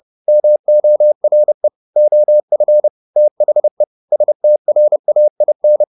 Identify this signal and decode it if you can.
Morse code, heard as a keyed tone, about 30 wpm, 600 Hz. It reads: MORE OF THE STRAIN
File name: the old man and the sea / the old man and the sea 1 30wpm